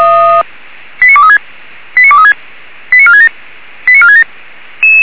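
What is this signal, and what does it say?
Five-tone selective call (ZVEI), heard as quick runs of five tones, tones of about 70 ms. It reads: 89236 89236 89467 89467
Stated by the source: Received with an SDR; ZVEI five-tone call 89378.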